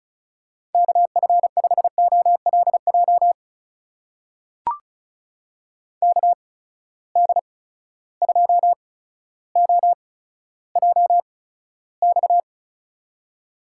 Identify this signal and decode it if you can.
Morse code, heard as a keyed tone, about 36 wpm, spaced as for 15 wpm, 700 Hz. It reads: KF5OLJ KD2OJX
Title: USA Call Signs - 6 Characters - 8x Character Spacing - 35wpm